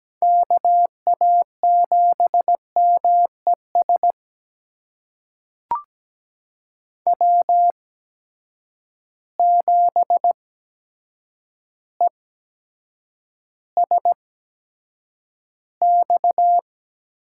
Morse code, heard as a keyed tone, about 17 wpm, spaced as for 8 wpm, 700 Hz. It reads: KA7MES W7ESX
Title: USA Call Signs - 8x Character Spacing - 17wpm